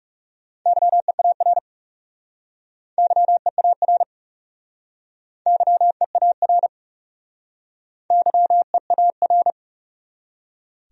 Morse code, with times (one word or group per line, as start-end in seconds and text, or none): 0.7-1.6 YEAR
3.0-4.0 YEAR
5.5-6.7 YEAR
8.1-9.5 YEAR